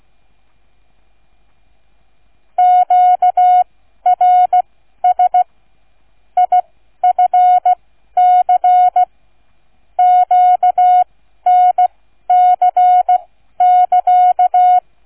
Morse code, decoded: QRS IFC QNC<KA>

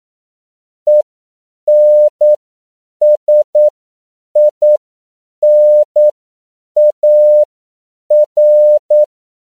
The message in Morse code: ENSINAR

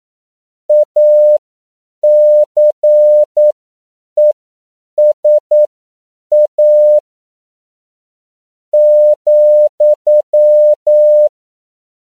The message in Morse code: ACESA,

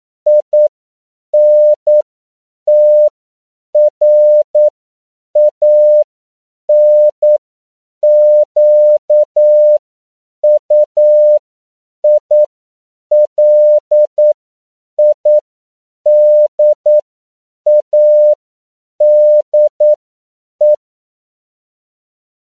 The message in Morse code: INTRANQUILIDADE

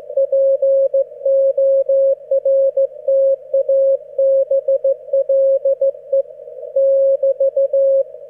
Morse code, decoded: PORTABLE <BT>

